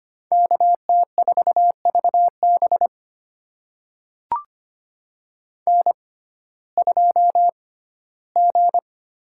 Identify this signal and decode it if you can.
Morse code, heard as a keyed tone, about 25 wpm, spaced as for 15 wpm, 700 Hz. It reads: KT4VB N2G
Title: USA Call Signs - 5 Characters - 6x Character Spacing - 25wpm